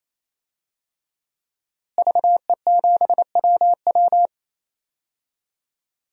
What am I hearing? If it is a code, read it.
Morse code, heard as a keyed tone, about 27 wpm, 700 Hz. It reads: VE7WW
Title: DXCC-100 Call Signs - 5 Characters - Repeated 3x - 28wpm